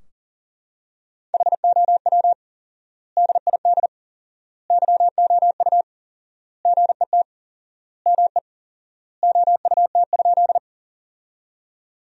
Morse code, heard as a keyed tone, about 40 wpm, 700 Hz. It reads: HOW DID YOU GET ME OUT?
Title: 4x Word-Spacing - Sentences from Top 200 Words - 40wpm